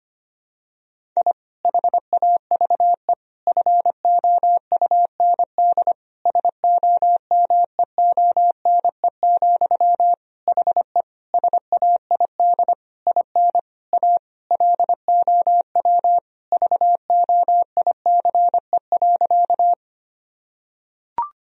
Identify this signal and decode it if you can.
Morse code, heard as a keyed tone, about 25 wpm, 700 Hz. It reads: I HAVE FOUND SOMEONE, HE SAID IN A LOW VOICE.
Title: Sentences from Top 1000 Words - Review Part 1 - 25wpm